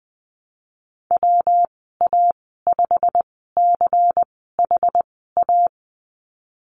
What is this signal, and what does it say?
Morse code, heard as a keyed tone, about 20 wpm, spaced as for 15 wpm, 700 Hz. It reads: WA5CHA
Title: 128 Lesson 21g - Number 6 - Call Signs 20wpm